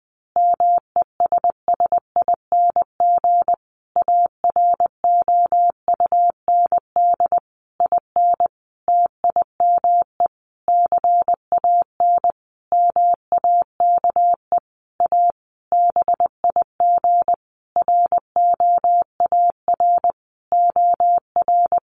Morse code, heard as a keyed tone, about 20 wpm, 700 Hz. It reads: MESSING AROUND IN TIME CAN MAKE A BIG ROAR OR
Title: A Sound Of Thunder 20wpm